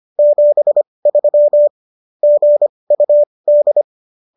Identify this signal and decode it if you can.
Morse code, heard as a keyed tone, about 25 wpm, spaced as for 20 wpm, 590 Hz. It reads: 73 GUD